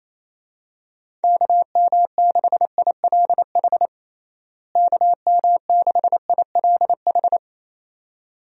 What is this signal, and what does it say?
Morse code, heard as a keyed tone, about 28 wpm, 700 Hz. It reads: KM6ILH KM6ILH
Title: USA Call Signs - 6 Characters - Repeated 2x - 28wpm